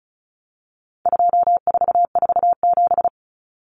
Morse code, heard as a keyed tone, about 34 wpm, 700 Hz. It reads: 2447
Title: Numbers - 4-Digit 35wpm